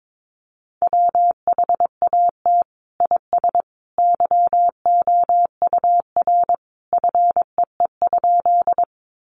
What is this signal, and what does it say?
Morse code, heard as a keyed tone, about 22 wpm, 700 Hz. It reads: WHAT IS YOUR FEE?
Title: Sentences from Top 1500 Words 22wpm